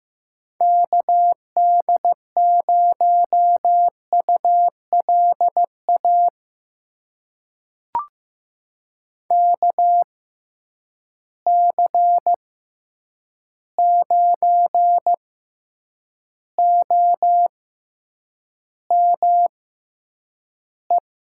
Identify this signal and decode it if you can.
Morse code, heard as a keyed tone, about 15 wpm, spaced as for 8 wpm, 700 Hz. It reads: KD0ULA KC9OME
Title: USA Call Signs - 6 Characters - 6x Character Spacing - 15wpm